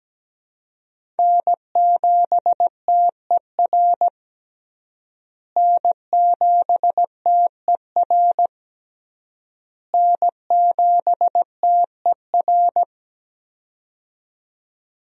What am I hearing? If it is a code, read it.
Morse code, heard as a keyed tone, about 17 wpm, 700 Hz. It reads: N7TER N7TER N7TER
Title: USA Call Signs - 5 Characters - Repeated 3x - 17wpm